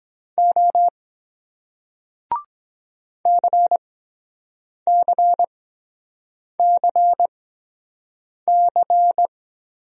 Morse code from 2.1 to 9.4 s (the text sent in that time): CCCC